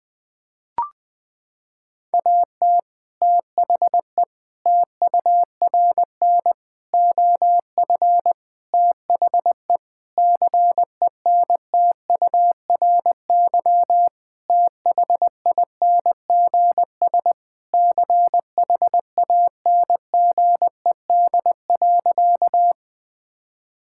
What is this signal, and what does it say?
Morse code, heard as a keyed tone, about 21 wpm, 700 Hz. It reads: AT THE TURN OF THE CENTURY THINGS CHANGED.
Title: Sentences with 4-Word Phrases - 20wpm